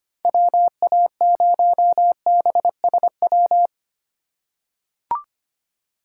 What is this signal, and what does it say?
Morse code, heard as a keyed tone, about 25 wpm, 700 Hz. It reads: WA0BSW
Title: USA Call Signs - 2x Character Spacing - 25wpm